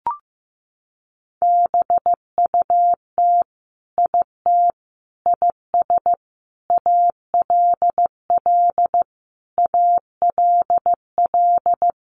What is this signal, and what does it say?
Morse code, heard as a keyed tone, about 15 wpm, 700 Hz. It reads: BUT IT IS ALL ALL
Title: Sentences from Top 300 Words 15wpm